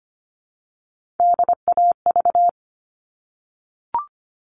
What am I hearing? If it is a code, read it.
Morse code, heard as a keyed tone, about 25 wpm, 700 Hz. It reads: DAV
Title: All Cut Numbers - 3-Digit 25wpm